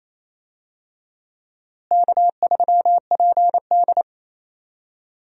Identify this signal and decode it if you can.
Morse code, heard as a keyed tone, about 28 wpm, 700 Hz. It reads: K3PD
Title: USA Call Signs - 4 Characters - Repeated 3x - 28wpm